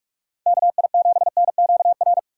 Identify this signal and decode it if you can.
Morse code, heard as a keyed tone, about 46 wpm, 700 Hz. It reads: KI7NQR
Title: USA Call Signs - 6 Characters - 4x Character Spacing - 45wpm